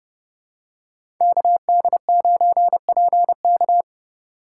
Morse code, heard as a keyed tone, about 29 wpm, 700 Hz. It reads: KD9PK
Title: USA Call Signs - 5 Characters - 8x Character Spacing - 30wpm